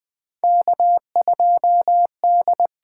Morse code, heard as a keyed tone, about 20 wpm, 700 Hz. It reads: K2D
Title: USA Call Signs - 8x Character Spacing - 20wpm